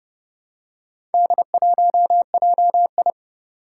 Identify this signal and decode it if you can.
Morse code, heard as a keyed tone, about 29 wpm, 700 Hz. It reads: D1JI
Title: Four Challenging Letter-Number Combinations 30wpm